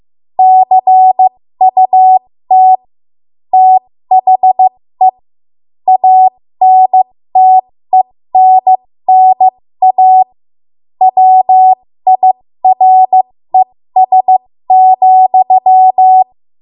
Morse code, heard as a keyed tone, about 15 wpm, spaced as for 13 wpm, 750 Hz. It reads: CUT THE ANTENNA WIRES,
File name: Morse / 13WPM